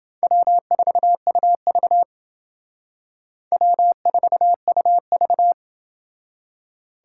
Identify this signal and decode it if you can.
Morse code, heard as a keyed tone, about 28 wpm, 700 Hz. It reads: W4UV W4UV